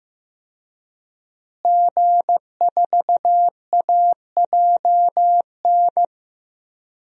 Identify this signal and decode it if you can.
Morse code, heard as a keyed tone, about 15 wpm, 700 Hz. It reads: G4AJN
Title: DXCC-100 Call Signs - 5 Characters - Repeated 2x - 15wpm